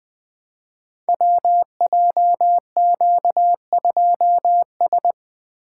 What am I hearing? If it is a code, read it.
Morse code, heard as a keyed tone, about 20 wpm, 700 Hz. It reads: WJQ2S
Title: Code Groups - 5 Letter:Numbers - 20wpm